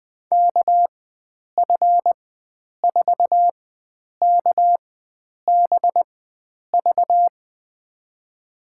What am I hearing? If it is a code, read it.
Morse code, heard as a keyed tone, about 21 wpm, spaced as for 9 wpm, 700 Hz. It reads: KF4KBV
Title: USA Call Signs - 6 Characters - 4x Character Spacing - 20wpm